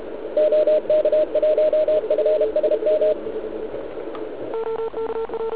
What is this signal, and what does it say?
Morse code, heard as a keyed tone, about 34 wpm, 610 Hz. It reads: OK1FSM
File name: ARCHIV / zvuk2 / CW RUN